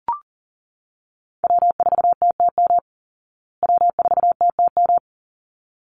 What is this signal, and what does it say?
Morse code, heard as a keyed tone, about 40 wpm, 700 Hz. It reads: W4TTM W4TTM
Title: Call Signs Repeated 3x - 40wpm